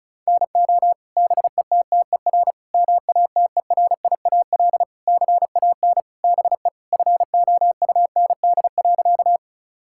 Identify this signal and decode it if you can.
Morse code, heard as a keyed tone, about 36 wpm, 690 Hz. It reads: NO BETTER MATERIAL CAN BE FOUND.
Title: Sentences from Top 1000 Words - Review Part 5 - 35wpm